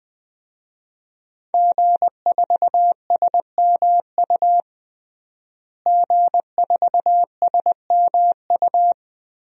Morse code, heard as a keyed tone, about 21 wpm, 700 Hz. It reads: G4SMU G4SMU
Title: DXCC-100 Call Signs - 5 Characters - Repeated 3x - 20wpm